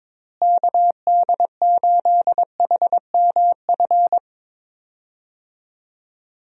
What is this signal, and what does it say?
Morse code, heard as a keyed tone, about 22 wpm, 700 Hz. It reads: KD8HMF
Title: USA Call Signs - 6 Characters - Repeated 2x - 22wpm